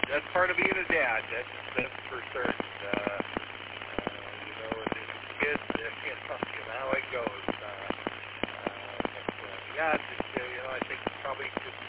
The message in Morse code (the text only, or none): none